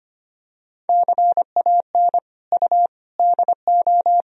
Morse code, heard as a keyed tone, about 25 wpm, 700 Hz. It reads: CAN U DO
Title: Sentences from Top 100 Words and Abbreviations 25wpm